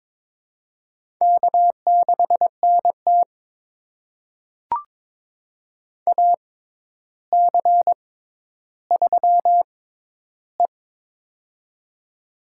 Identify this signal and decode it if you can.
Morse code, heard as a keyed tone, about 22 wpm, spaced as for 11 wpm, 700 Hz. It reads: K6NT AC3E
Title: USA Call Signs - 4 Characters - 6x Character Spacing - 22wpm